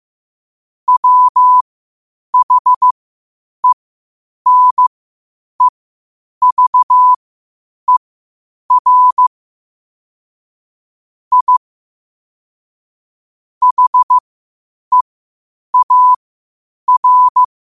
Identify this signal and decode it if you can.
Morse code, heard as a keyed tone, about 15 wpm, spaced as for 8 wpm, 1000 Hz. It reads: WHENEVER I HEAR
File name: qotdfiles / 8wpm